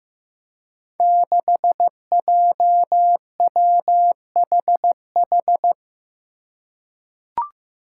Morse code, heard as a keyed tone, about 15 wpm, 700 Hz. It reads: 6JWHH
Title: Five Challenging Letter-Number Combinations 15wpm